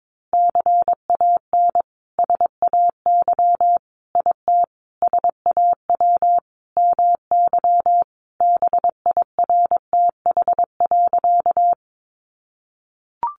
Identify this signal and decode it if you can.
Morse code, heard as a keyed tone, about 22 wpm, 700 Hz. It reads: CAN SAY IT SAW MY BIRTH.
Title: Sentences from Top 800 Words 22wpm